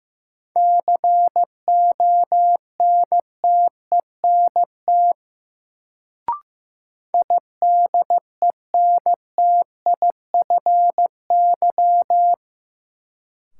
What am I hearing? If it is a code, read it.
Morse code, heard as a keyed tone, about 15 wpm, 700 Hz. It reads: CONTENT IDENTIFY
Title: Sentences from Top 900 Words - New Word Drill - 15wpm